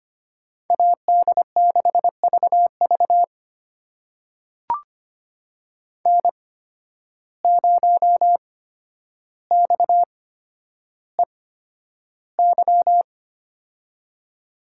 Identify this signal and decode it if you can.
Morse code, heard as a keyed tone, about 25 wpm, spaced as for 11 wpm, 700 Hz. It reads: AD6VV N0XEY
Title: USA Call Signs - 5 Characters - 8x Character Spacing - 25wpm